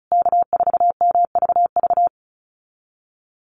Morse code, heard as a keyed tone, about 34 wpm, 700 Hz. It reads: K4MVV
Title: USA Call Signs Repeated 3x 35wpm